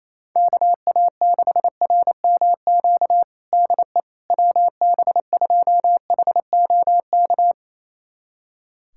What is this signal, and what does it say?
Morse code, heard as a keyed tone, about 27 wpm, 700 Hz. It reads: KA6RMQ DE WB2HOK
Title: USA Callsign DE Callsign - 6 Characters - 28wpm